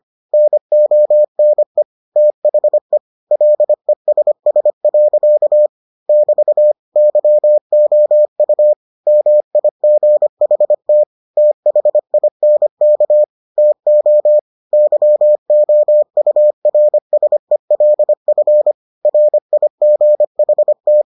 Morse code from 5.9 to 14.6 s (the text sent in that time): <BT> YOU MIGHT THINK TO